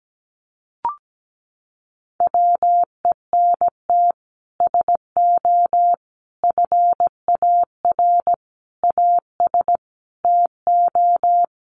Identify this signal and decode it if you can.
Morse code, heard as a keyed tone, about 17 wpm, 700 Hz. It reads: WENT SO FAR AS TO
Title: Sets of 5 Words Encore 17wpm